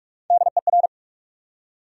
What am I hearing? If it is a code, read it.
Morse code, heard as a keyed tone, about 46 wpm, 700 Hz. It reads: DER
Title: USA Call Sign Postfixes - 3 Letters 45wpm